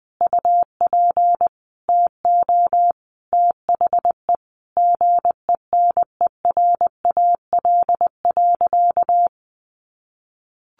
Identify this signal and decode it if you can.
Morse code, heard as a keyed tone, about 20 wpm, 700 Hz. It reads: UP TO THE GENERAL.
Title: Sentences from Top 1500 Words 20wpm